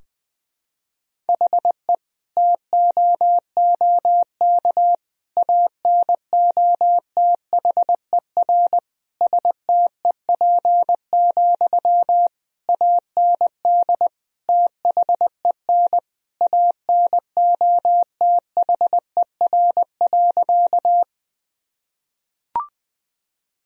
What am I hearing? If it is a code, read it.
Morse code, heard as a keyed tone, about 21 wpm, 700 Hz. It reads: HE TOOK ANOTHER STEP, AND THEN ANOTHER.
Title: Sentences from Top 1000 Words - Review Part 2 - 20wpm